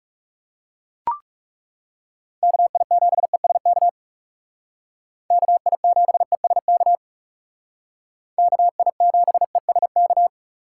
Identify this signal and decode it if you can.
Morse code, heard as a keyed tone, about 38 wpm, 700 Hz. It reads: KI7ESK KI7ESK KI7ESK